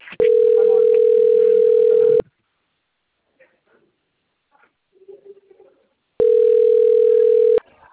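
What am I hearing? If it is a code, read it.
Ringback tone (North America), 440 + 480 Hz.
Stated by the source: Recorded outgoing phone call on a PBX system in Piracicaba.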